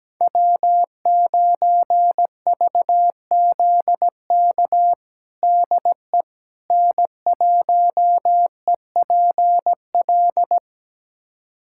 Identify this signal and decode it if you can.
Morse code, heard as a keyed tone, about 17 wpm, 700 Hz. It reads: W9VZK DE N1EPL
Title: USA Callsign DE Callsign - 5 Characters - 17wpm